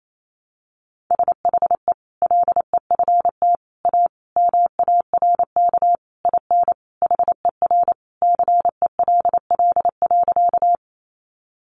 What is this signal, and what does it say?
Morse code, heard as a keyed tone, about 28 wpm, 700 Hz. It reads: SHE LEFT A MARK IN HER CELL.
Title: Sentences from Top 1100 Words 28wpm